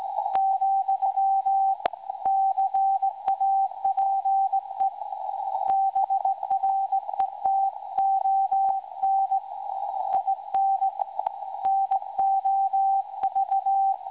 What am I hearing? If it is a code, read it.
Morse code, heard as a keyed tone, about 18 wpm, 780 Hz. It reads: , CAPE BRETON IN NOV
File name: WebSDR records / SAQ recording 17.2kHz